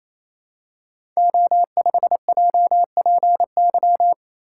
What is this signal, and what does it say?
Morse code, heard as a keyed tone, about 27 wpm, 700 Hz. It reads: O5JPY
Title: Code Groups - 5 Letter:Numbers - 28wpm